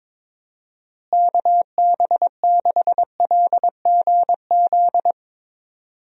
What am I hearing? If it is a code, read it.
Morse code, heard as a keyed tone, about 22 wpm, 700 Hz. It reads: KB6LGZ
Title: USA Call Signs - 6 Characters - Repeated 2x - 22wpm